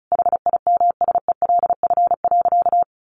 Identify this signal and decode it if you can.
Morse code, heard as a keyed tone, about 34 wpm, 700 Hz. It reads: HIMSELF.
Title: Sentences from Top 1500 Words 35wpm